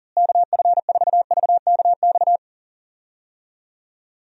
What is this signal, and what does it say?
Morse code, heard as a keyed tone, about 41 wpm, 700 Hz. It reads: KF4VKX
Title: USA Call Signs - 6 Characters - Repeated 2x - 40wpm